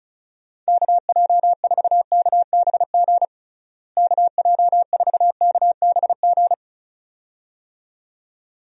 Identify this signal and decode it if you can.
Morse code, heard as a keyed tone, about 36 wpm, 700 Hz. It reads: KJ4KBG KJ4KBG
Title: USA Call Signs - 6 Characters - Repeated 2x - 35wpm